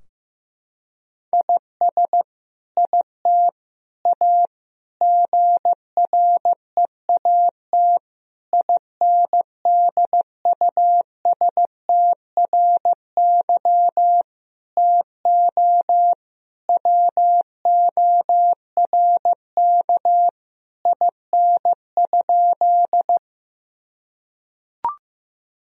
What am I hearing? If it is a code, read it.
Morse code, heard as a keyed tone, about 15 wpm, 700 Hz. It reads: IS IT A GREAT INDUSTRY TO WORK IN?
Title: Sentences from Top 1000 Words - Review Part 4 - 15wpm